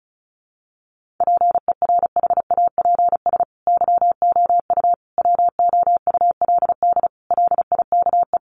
Morse code, heard as a keyed tone, about 34 wpm, 700 Hz. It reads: PERHAPS YOU WOULD LIKE
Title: Sentences from Top 500 Words 35wpm